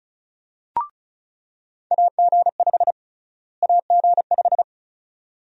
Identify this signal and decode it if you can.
Morse code, heard as a keyed tone, about 36 wpm, 700 Hz. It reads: AG5 AG5
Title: USA Call Sign Prefixes - Repeated 2x - 35wpm